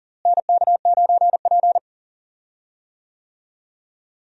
Morse code, heard as a keyed tone, about 40 wpm, 700 Hz. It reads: NK9P
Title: USA Call Signs - 4 Characters - Repeated 2x - 40wpm